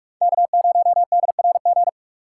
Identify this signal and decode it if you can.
Morse code, heard as a keyed tone, about 44 wpm, 700 Hz. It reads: K0DRG